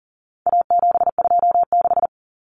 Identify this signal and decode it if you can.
Morse code, heard as a keyed tone, about 37 wpm, 700 Hz. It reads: A726
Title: Common Cut Numbers - 4-Digit 40wpm